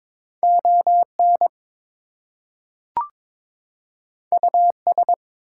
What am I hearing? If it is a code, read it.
Morse code, heard as a keyed tone, about 22 wpm, 700 Hz. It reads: ON US
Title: Two Letter Words - 22wpm